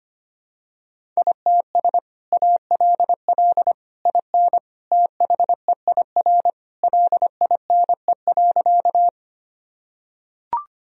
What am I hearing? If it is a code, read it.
Morse code, heard as a keyed tone, about 25 wpm, 700 Hz. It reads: ITS ALL IN THEIR LINE.
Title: Sentences from Top 1000 Words - Review Part 1 - 25wpm